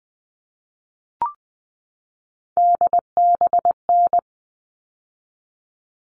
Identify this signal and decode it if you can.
Morse code, heard as a keyed tone, about 20 wpm, 700 Hz. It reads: DBN
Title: All Cut Numbers - 3-Digit 20wpm